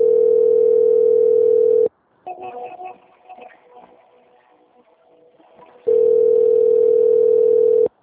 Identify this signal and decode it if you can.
Ringback tone (North America), 440 + 480 Hz.